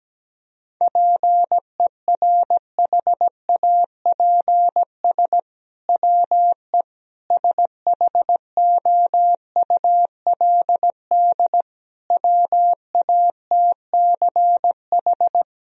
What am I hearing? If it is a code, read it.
Morse code, heard as a keyed tone, about 17 wpm, 700 Hz. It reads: PERHAPS WE SHOULD WATCH
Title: Sentences from Top 1000 Words - Review Part 4 - 17wpm